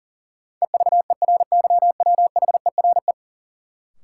Morse code, heard as a keyed tone, about 40 wpm, 700 Hz. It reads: EVERYWHERE
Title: Sentences from Top 1600 Words - New Word Drill - 40wpm